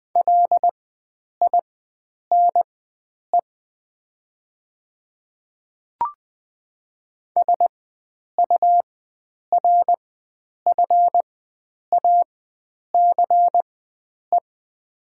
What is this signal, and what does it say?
Morse code, heard as a keyed tone, about 20 wpm, spaced as for 9 wpm, 700 Hz. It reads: LINE SURFACE
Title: 4x Character Spacing - Top 500 Words - 20wpm